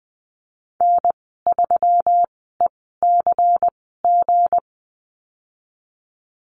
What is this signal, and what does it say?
Morse code, heard as a keyed tone, about 20 wpm, spaced as for 14 wpm, 700 Hz. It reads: N3ECG
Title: 200 Lesson 32g - Letter k - Call Signs 20wpm